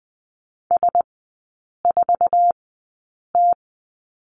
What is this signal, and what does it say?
Morse code, heard as a keyed tone, about 20 wpm, spaced as for 8 wpm, 700 Hz. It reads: S4T
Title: 047 Lesson 9e - Number 4 - Three Characters 20wpm